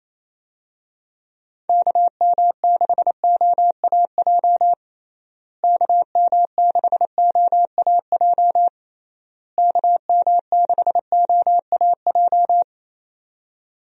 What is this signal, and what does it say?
Morse code, heard as a keyed tone, about 28 wpm, 700 Hz. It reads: KM6OAJ KM6OAJ KM6OAJ
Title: USA Call Signs - 6 Characters - Repeated 3x - 28wpm